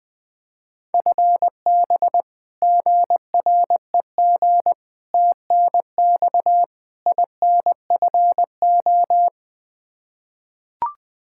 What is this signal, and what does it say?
Morse code, heard as a keyed tone, about 20 wpm, 700 Hz. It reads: FB GREG TNX INFO
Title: Practice QSO - Exchange 3 - More Info - 20wpm